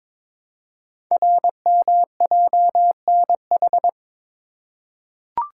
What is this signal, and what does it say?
Morse code, heard as a keyed tone, about 23 wpm, 700 Hz. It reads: RMJNH
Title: Code Groups - 5 Letters - 22wpm